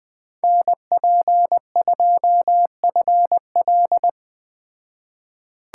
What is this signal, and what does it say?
Morse code, heard as a keyed tone, about 20 wpm, 700 Hz. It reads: NP2FL